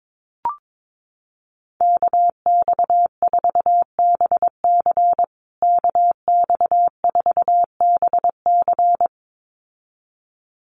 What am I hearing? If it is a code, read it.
Morse code, heard as a keyed tone, about 22 wpm, 700 Hz. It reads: KX4BC KX4BC
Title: USA Call Signs Repeated 2x 22wpm